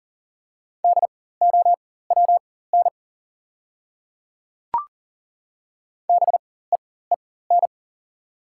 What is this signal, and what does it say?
Morse code, heard as a keyed tone, about 40 wpm, spaced as for 19 wpm, 690 Hz. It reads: DOWN BEEN
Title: 4x Character Spacing - Top 100 Words - 40wpm